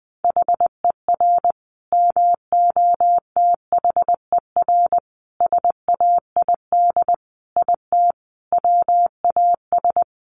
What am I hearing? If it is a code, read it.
Morse code, heard as a keyed tone, about 20 wpm, 700 Hz. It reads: HER MOTHER SAID IT WAS
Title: Sentences from Top 1200 Words 20wpm